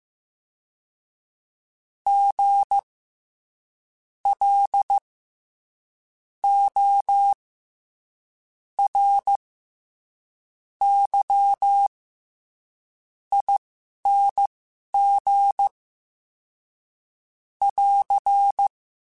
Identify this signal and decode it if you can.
Morse code, heard as a keyed tone, about 15 wpm, spaced as for 6 wpm, 800 Hz. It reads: GLORYING <AR>
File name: Special INGs-mp3s / Special INGs-mp3s / SlowToFaster ing @10